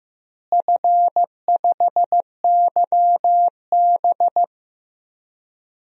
Morse code, heard as a keyed tone, about 15 wpm, 700 Hz. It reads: F5YB
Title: DXCC-100 Call Signs - 4 Characters - Repeated 3x - 15wpm